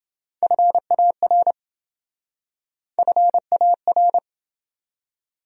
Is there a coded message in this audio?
Yes: Morse code from 0.4 to 4.2 s, heard as a keyed tone, about 28 wpm, 700 Hz. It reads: FAR FAR